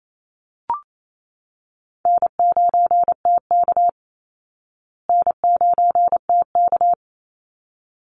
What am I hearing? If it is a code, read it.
Morse code, heard as a keyed tone, about 28 wpm, 700 Hz. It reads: N9TK N9TK
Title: Easy Call Signs Repeated 2x - 28wpm